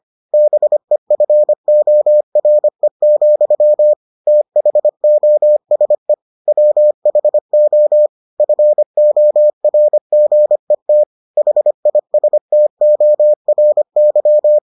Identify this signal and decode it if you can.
Morse code, heard as a keyed tone, about 25 wpm, 600 Hz. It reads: BEFORE, THOSE WHO FORGET HISTORY